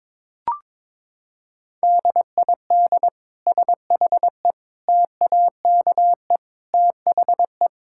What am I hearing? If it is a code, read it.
Morse code, heard as a keyed tone, about 22 wpm, 700 Hz. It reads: DID SHE TAKE THE
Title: Sentences from Top 1000 Words - Review Part 6 - 22wpm